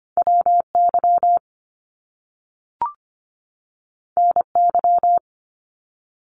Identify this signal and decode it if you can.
Morse code, heard as a keyed tone, about 25 wpm, 700 Hz. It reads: WY NY
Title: US States 25wpm